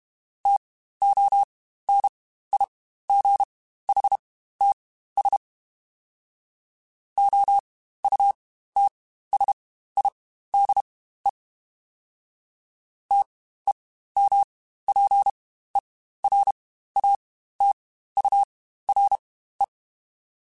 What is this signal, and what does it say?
Morse code, heard as a keyed tone, about 32 wpm, spaced as for 15 wpm, 800 Hz. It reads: TONIGHTS OUTSIDE TEMPERATURE
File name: Spaced Characters / Spaced Characters-mp3s / Weather5 22 sp @30